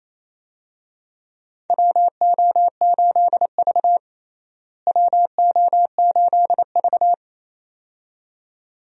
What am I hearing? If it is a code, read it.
Morse code, heard as a keyed tone, about 28 wpm, 700 Hz. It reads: WO8V WO8V
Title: USA Call Signs - 4 Characters - Repeated 2x - 28wpm